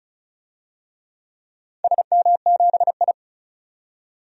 Morse code, heard as a keyed tone, about 36 wpm, 700 Hz. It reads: SM7I
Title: DXCC-100 Call Signs - 4 Characters - 35wpm